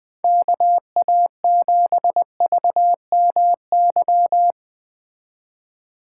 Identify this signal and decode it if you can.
Morse code, heard as a keyed tone, about 20 wpm, 700 Hz. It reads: KA7VMY
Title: Speed Racing - USA Call Signs 17wpm